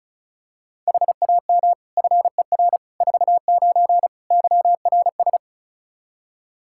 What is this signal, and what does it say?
Morse code, heard as a keyed tone, about 36 wpm, 700 Hz. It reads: HAM FER 49 YRS
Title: K1USN SST QSO Practice 35wpm